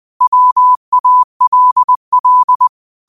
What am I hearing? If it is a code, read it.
Morse code, heard as a keyed tone, about 20 wpm, 1000 Hz. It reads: WALL